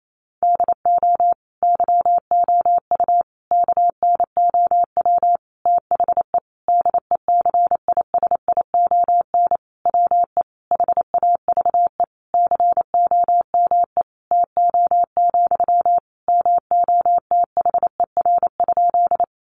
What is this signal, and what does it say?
Morse code, heard as a keyed tone, about 28 wpm, 700 Hz. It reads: DO YOU KNOW THE DECISION WE HAVE COME TO, MOTHER?
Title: Sentences from Top 1000 Words 28wpm